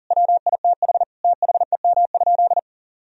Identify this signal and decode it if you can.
Morse code, heard as a keyed tone, about 37 wpm, 700 Hz. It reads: WITH THEM?
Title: Sentences from Top 1000 Words - Review Part 3 - 40wpm